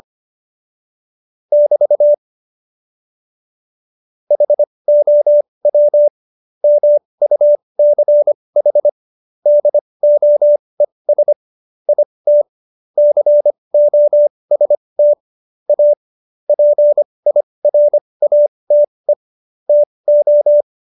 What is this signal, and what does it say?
Morse code, heard as a keyed tone, about 25 wpm, spaced as for 20 wpm, 590 Hz. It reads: <BT> HOW MUCH DOES IT COST A PIRATE TO
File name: Jokes-25-char-20-eff